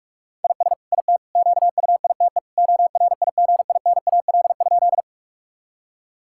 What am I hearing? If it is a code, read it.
Morse code, heard as a keyed tone, about 46 wpm, 700 Hz. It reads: IS IT QUITE ORIGINAL?
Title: Sentences from Top 1000 Words - Review Part 6 - 45wpm